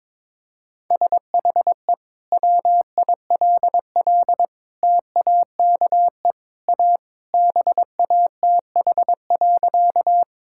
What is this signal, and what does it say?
Morse code, heard as a keyed tone, about 22 wpm, 700 Hz. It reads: SHE WILL TAKE A BATH.
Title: Sentences from Top 1600 Words 22wpm